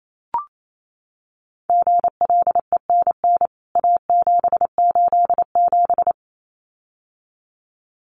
Morse code, heard as a keyed tone, about 28 wpm, 700 Hz. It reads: GLENN A787
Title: CWT Exchange 28wpm